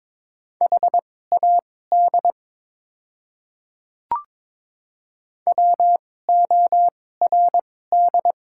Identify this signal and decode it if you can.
Morse code, heard as a keyed tone, about 23 wpm, spaced as for 16 wpm, 690 Hz. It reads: HAD WORD
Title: 2x Character Spacing - Top 100 Words - 22wpm